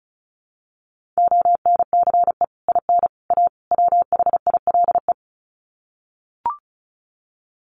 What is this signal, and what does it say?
Morse code, heard as a keyed tone, about 34 wpm, 700 Hz. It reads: ONCE IN A WHILE
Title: Sets of 4 Words 35wpm